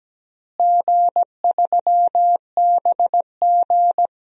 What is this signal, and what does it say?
Morse code, heard as a keyed tone, about 17 wpm, 700 Hz. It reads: G3BG
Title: DXCC-100 Call Signs - 4 Characters - Repeated 2x - 17wpm